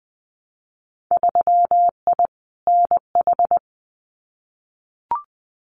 Morse code, heard as a keyed tone, about 20 wpm, 700 Hz. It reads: 3I NH
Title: Winter Field Day Exchanges Encore 20wpm